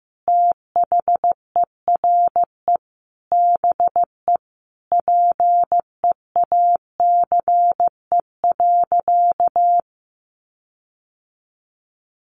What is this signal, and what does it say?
Morse code, heard as a keyed tone, about 15 wpm, 700 Hz. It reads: THERE BE PEACE.
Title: Sentences from Top 1500 Words 15wpm